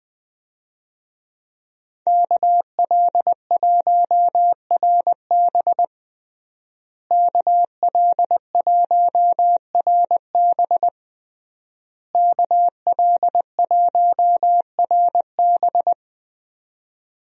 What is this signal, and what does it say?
Morse code, heard as a keyed tone, about 20 wpm, 700 Hz. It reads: KL1RB KL1RB KL1RB
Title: USA Call Signs - 5 Characters - Repeated 3x - 20wpm